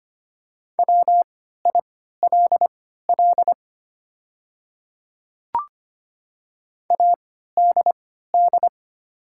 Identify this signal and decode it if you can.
Morse code, heard as a keyed tone, about 25 wpm, spaced as for 14 wpm, 700 Hz. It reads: WILL ADD
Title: 3x Character Spacing - Top 200 Words - 25wpm